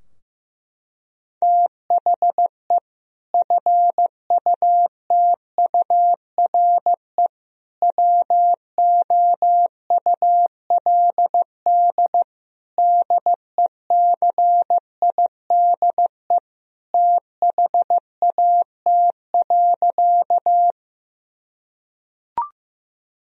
Morse code, heard as a keyed tone, about 16 wpm, 700 Hz. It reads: THE FUTURE WOULD DECIDE THAT.
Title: Sentences from Top 1000 Words - Review Part 3 - 15wpm